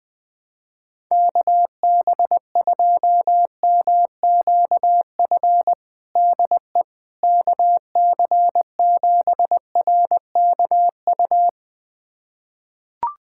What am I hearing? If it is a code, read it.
Morse code, heard as a keyed tone, about 20 wpm, 700 Hz. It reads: KB2MQF DE KC7RKU
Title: USA Callsign DE Callsign 20wpm